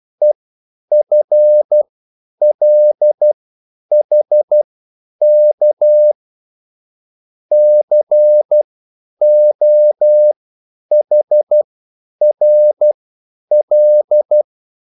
Morse code, heard as a keyed tone, about 12 wpm, spaced as for 9 wpm, 600 Hz. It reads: EFLHK COHRL
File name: QSL / CW lesson